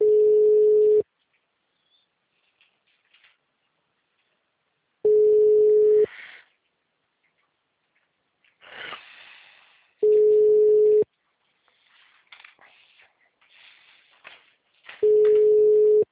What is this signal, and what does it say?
Ringback tone (Europe, 425 Hz).